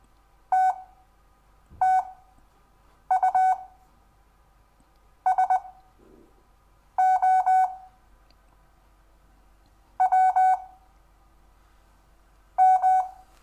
Morse code, heard as a keyed tone, about 20 wpm, spaced as for 6 wpm, 750 Hz. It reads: TTU SO W M
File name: cw / Lesson6 WI